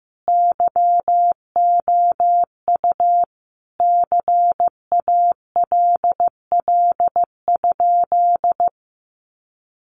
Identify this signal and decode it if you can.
Morse code, heard as a keyed tone, about 15 wpm, 700 Hz. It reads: YOU CALL?
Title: Sentences from Top 100 Words 15wpm